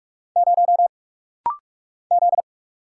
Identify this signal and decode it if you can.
Morse code, heard as a keyed tone, about 45 wpm, 700 Hz. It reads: 0 Z
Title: Single Letter-Number - Rapid-Fire 45wpm